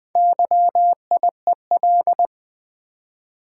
Morse code, heard as a keyed tone, about 20 wpm, 700 Hz. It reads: YIEL
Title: Build A Word - 1 Letter At A Time - 5-Letter Words - 20wpm